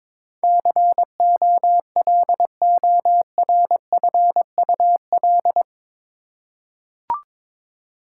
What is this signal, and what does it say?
Morse code, heard as a keyed tone, about 22 wpm, 700 Hz. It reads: COLORFUL
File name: practice / 22wpm / Eight Letter Words - 22wpm